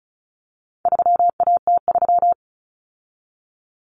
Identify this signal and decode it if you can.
Morse code, heard as a keyed tone, about 34 wpm, 700 Hz. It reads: 3AT3
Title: Common Cut Numbers - 4-Digit 35wpm